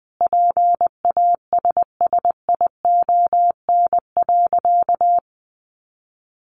Morse code, heard as a keyed tone, about 20 wpm, 700 Hz. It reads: PASSION.
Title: Sentences from Top 1300 Words 20wpm